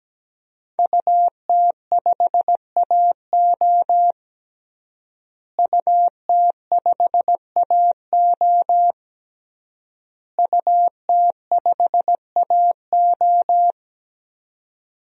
Morse code, heard as a keyed tone, about 17 wpm, 700 Hz. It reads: UT5AO UT5AO UT5AO
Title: DXCC-100 Call Signs - 5 Characters - Repeated 3x - 17wpm